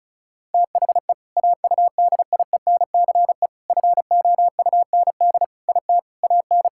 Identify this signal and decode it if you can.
Morse code, heard as a keyed tone, about 36 wpm, 700 Hz. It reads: THE AUDIENCE FOUND IT AN
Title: Sentences from Top 1000 Words - Review Part 6 - 35wpm